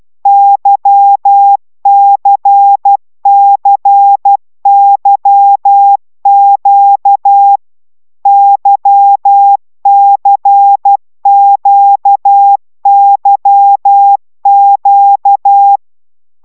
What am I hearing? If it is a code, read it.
Morse code, heard as a keyed tone, about 12 wpm, 800 Hz. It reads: YCCYQ YCQYQ
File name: taviro tanulas / cw 75 15 10